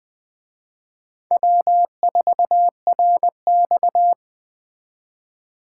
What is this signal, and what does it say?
Morse code, heard as a keyed tone, about 20 wpm, 700 Hz. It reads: W4RX
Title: Speed Racing - USA 1x2 Call Signs 17wpm